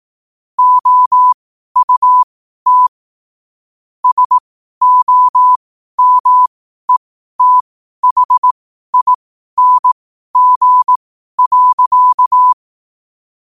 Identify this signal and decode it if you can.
Morse code, heard as a keyed tone, about 18 wpm, spaced as for 12 wpm, 1000 Hz. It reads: OUT SOMETHING.
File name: qotdfiles / 12wpm